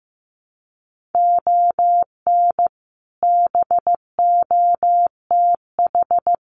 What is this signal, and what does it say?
Morse code, heard as a keyed tone, about 15 wpm, 700 Hz. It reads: ON BOTH
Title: Sets of 4 Words 15wpm